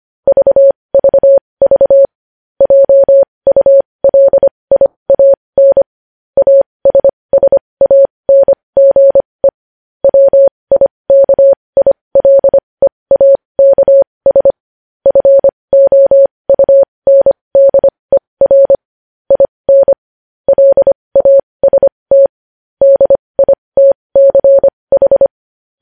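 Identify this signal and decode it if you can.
Morse code, heard as a keyed tone, about 25 wpm, spaced as for 20 wpm, 550 Hz. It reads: VVV JULIAN ASSANGE WIKILEAKS FOUNDER IN LAST DITCH
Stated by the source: Morse practice of news headlines at 20 wpm.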